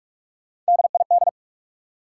Morse code, heard as a keyed tone, about 60 wpm, spaced as for 53 wpm, 700 Hz. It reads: DID
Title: Speed-Racing Top 180 QSO Elements 45wpm